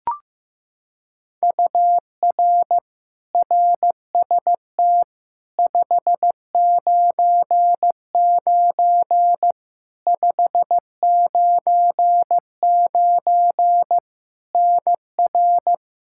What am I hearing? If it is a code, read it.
Morse code, heard as a keyed tone, about 15 wpm, 700 Hz. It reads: UR RST 599 599 NR
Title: Practice QSO - Exchange 1 - Beginning - 15wpm